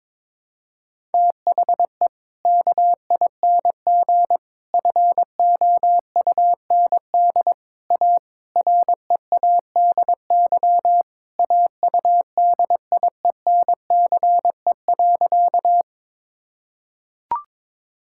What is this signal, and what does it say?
Morse code, heard as a keyed tone, about 22 wpm, 700 Hz. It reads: THE KING FOUND A READY AUDIENCE.
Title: Sentences from Top 1000 Words - Review Part 5 - 22wpm